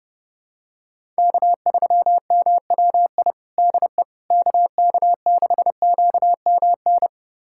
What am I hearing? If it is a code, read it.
Morse code, heard as a keyed tone, about 30 wpm, 700 Hz. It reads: K3MWI DE KK6QMN
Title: USA Callsign DE Callsign 30wpm